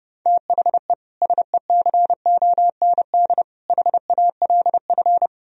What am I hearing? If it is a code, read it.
Morse code, heard as a keyed tone, about 40 wpm, 700 Hz. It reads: THE SECOND HALF